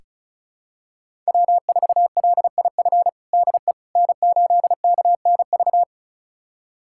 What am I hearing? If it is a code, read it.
Morse code, heard as a keyed tone, about 34 wpm, 700 Hz. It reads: W4LIF DE N8KNV
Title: USA Callsign DE Callsign - 5 Characters - 35wpm